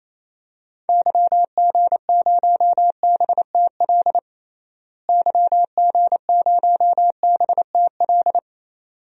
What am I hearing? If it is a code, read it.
Morse code, heard as a keyed tone, about 28 wpm, 700 Hz. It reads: YG0BTL YG0BTL
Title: DXCC-100 Call Signs - 6 Characters - Repeated 2x - 28wpm